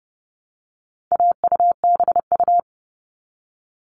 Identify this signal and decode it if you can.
Morse code, heard as a keyed tone, about 30 wpm, 700 Hz. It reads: AUBU
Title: All Cut Numbers - 4-Digit 30wpm